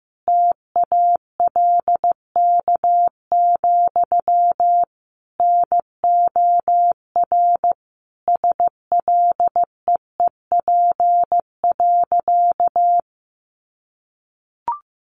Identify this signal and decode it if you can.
Morse code, heard as a keyed tone, about 15 wpm, 700 Hz. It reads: TALK, NOR SLEEP.
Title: Sentences from Top 1500 Words 15wpm